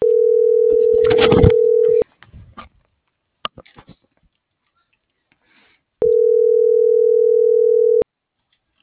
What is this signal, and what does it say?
Ringback tone (North America), 440 + 480 Hz.